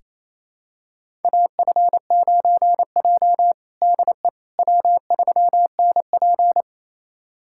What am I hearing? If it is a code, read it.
Morse code, heard as a keyed tone, about 28 wpm, 700 Hz. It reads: AF9J DE W3NP
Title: USA Callsign DE Callsign - 4 Characters - 28wpm